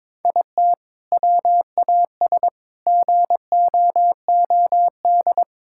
Morse code, heard as a keyed tone, about 21 wpm, 700 Hz. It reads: IT WAS GOOD
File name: practice / 22wpm / Sentences with 3-Word Phrases - 22wpm